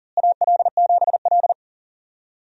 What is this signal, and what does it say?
Morse code, heard as a keyed tone, about 40 wpm, 700 Hz. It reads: AL7L
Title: USA Call Signs - 4 Characters - Repeated 3x - 40wpm